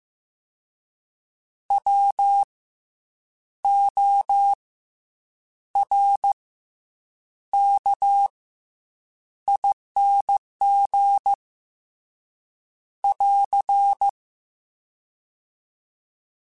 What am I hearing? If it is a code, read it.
Morse code, heard as a keyed tone, about 15 wpm, 800 Hz. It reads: W O R K ING <AR>
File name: Special INGs / Special INGs-mp3s / SlowToFaster ing @13